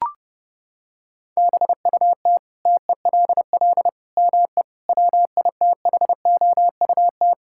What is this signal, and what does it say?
Morse code, heard as a keyed tone, about 30 wpm, 700 Hz. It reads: BUT TELL ME WITHOUT
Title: Sentences from Top 1000 Words - Review Part 6 - 30wpm